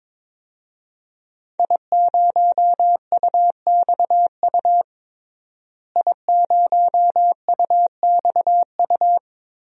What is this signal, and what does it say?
Morse code, heard as a keyed tone, about 22 wpm, 700 Hz. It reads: I0UXU I0UXU
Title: DXCC-100 Call Signs - 5 Characters - Repeated 2x - 22wpm